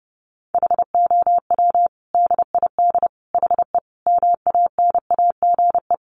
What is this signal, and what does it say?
Morse code, heard as a keyed tone, about 28 wpm, 700 Hz. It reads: HOW DID HE MANAGE
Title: Sentences from Top 1400 Words 30wpm